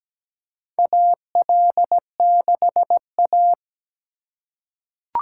Morse code, heard as a keyed tone, about 17 wpm, 700 Hz. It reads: AL6A
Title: USA Call Signs - 4 Characters - 4x Character Spacing - 17wpm